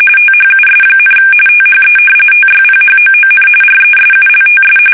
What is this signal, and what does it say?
RTTY, about 45.45 baud. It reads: THUS, THE TYPHOON RESULTS
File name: fsk850 / battleshipnj